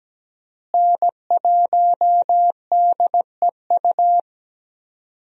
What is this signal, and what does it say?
Morse code, heard as a keyed tone, about 17 wpm, 700 Hz. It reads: N1DEU
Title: USA Call Signs - 5 Characters - Repeated 2x - 17wpm